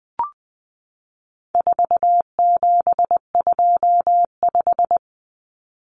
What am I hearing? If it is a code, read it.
Morse code, heard as a keyed tone, about 20 wpm, 700 Hz. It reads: 4725
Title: Numbers - 4-Digit 20wpm